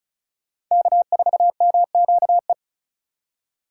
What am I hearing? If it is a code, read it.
Morse code, heard as a keyed tone, about 44 wpm, 700 Hz. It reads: K4MQE
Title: Speed Racing - USA 1x3 Call Signs 35wpm